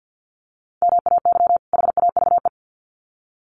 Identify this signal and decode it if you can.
Morse code, heard as a keyed tone, about 48 wpm, 700 Hz. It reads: MAY HAVE
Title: Sets of 2 Words 50wpm